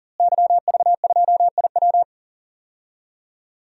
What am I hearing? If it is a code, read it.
Morse code, heard as a keyed tone, about 40 wpm, 700 Hz. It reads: YV2IW
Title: DXCC-100 Call Signs - 5 Characters - Repeated 2x - 40wpm